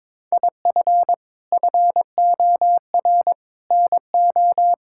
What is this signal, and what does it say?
Morse code, heard as a keyed tone, about 22 wpm, 700 Hz. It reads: IF FOR NO
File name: practice / 22wpm / Sets of 7 Words - 22wpm